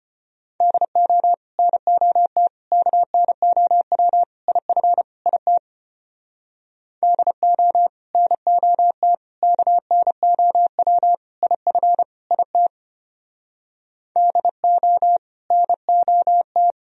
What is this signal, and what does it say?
Morse code, heard as a keyed tone, about 31 wpm, 700 Hz. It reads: DO NOT KNOW IF IT DO NOT KNOW IF IT DO NOT
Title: Speed-Racing - Sets of 5 Words 25wpm